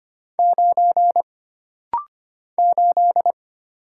Morse code, heard as a keyed tone, about 25 wpm, 700 Hz. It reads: 9 8
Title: Single Letter-Number - Rapid-Fire 25wpm